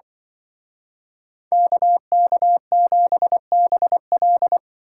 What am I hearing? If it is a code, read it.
Morse code, heard as a keyed tone, about 24 wpm, 700 Hz. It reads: KK7BL
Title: Speed Racing - USA 2x2 Call Signs 20wpm